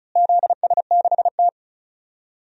Morse code, heard as a keyed tone, about 34 wpm, 700 Hz. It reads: ZS6T
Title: DXCC-100 Call Signs - 4 Characters - Repeated 3x - 35wpm